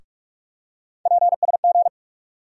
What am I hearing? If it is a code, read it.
Morse code, heard as a keyed tone, about 44 wpm, 700 Hz. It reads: PSG
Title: USA Call Sign Postfixes - 3 Letters 45wpm